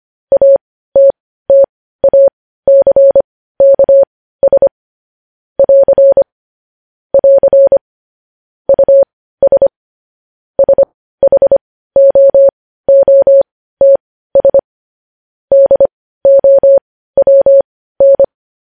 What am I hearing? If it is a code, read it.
Morse code, heard as a keyed tone, about 25 wpm, spaced as for 15 wpm, 550 Hz. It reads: ATTACKS <AR> <AR> US SHOOTS DOWN
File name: podcast / News headlines-1x-25x15